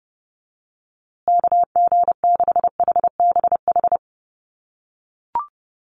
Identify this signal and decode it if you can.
Morse code, heard as a keyed tone, about 30 wpm, 700 Hz. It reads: KG6HBH
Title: USA Call Signs Repeated 2x 30wpm